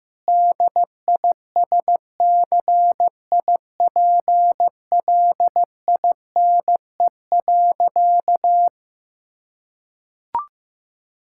Morse code, heard as a keyed tone, about 15 wpm, 700 Hz. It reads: DISCIPLINE.
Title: Sentences from Top 1000 Words - Review Part 6 - 15wpm